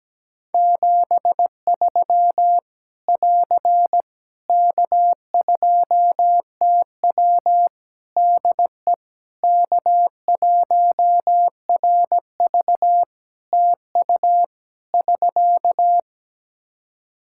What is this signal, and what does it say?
Morse code, heard as a keyed tone, about 17 wpm, 700 Hz. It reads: 73 <AR> K2TW DE K1RV TU <SK>
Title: K1USN SST QSO Practice 17wpm